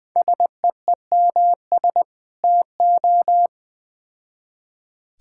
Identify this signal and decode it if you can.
Morse code, heard as a keyed tone, about 21 wpm, 700 Hz. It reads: SEEMS TO
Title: Build A Phrase - 3-Word Phrases - 20wpm